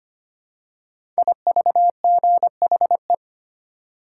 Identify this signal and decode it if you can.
Morse code, heard as a keyed tone, about 25 wpm, 700 Hz. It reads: IVGHE
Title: Code Groups - 5 Letters - 25wpm